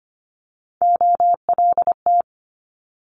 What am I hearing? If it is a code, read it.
Morse code, heard as a keyed tone, about 25 wpm, 700 Hz. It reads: OLT
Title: Multi-letter Phonemes Quick Fire 25wpm